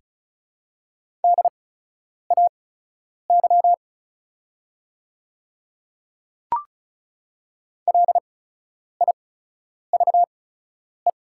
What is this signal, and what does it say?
Morse code, heard as a keyed tone, about 36 wpm, spaced as for 10 wpm, 690 Hz. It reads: DAY LIVE
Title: 8x Character Spacing - Top 500 Words - 35wpm